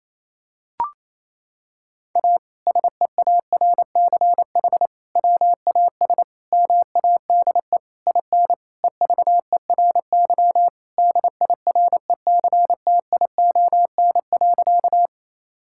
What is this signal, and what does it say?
Morse code, heard as a keyed tone, about 27 wpm, 700 Hz. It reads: A SEARCH WAS MADE IN EVERY DIRECTION.
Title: Sentences from Top 1000 Words - Review Part 5 - 28wpm